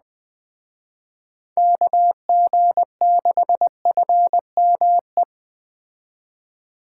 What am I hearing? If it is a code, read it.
Morse code, heard as a keyed tone, about 19 wpm, 700 Hz. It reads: KG6FME